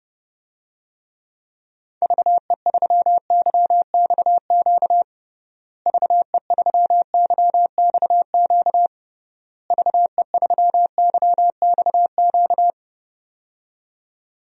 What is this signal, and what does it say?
Morse code, heard as a keyed tone, about 30 wpm, 700 Hz. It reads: VE3YXQ VE3YXQ VE3YXQ
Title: DXCC-100 Call Signs - 6 Characters - Repeated 3x - 30wpm